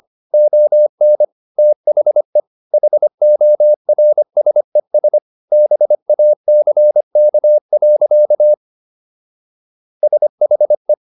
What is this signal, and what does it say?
Morse code, heard as a keyed tone, about 25 wpm, 600 Hz. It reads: ON THE HORSES BACK. SHE